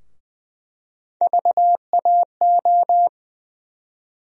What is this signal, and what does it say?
Morse code, heard as a keyed tone, about 20 wpm, 700 Hz. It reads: VAO
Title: USA Call Sign Postfixes - 3 Letters 20wpm